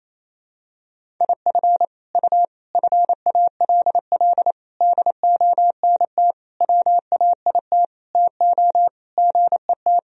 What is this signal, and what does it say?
Morse code, heard as a keyed tone, about 28 wpm, 700 Hz. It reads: IF U FALL DONT WAIT TO GET
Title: Sentences from Top 500 Words and Abbreviations 28wpm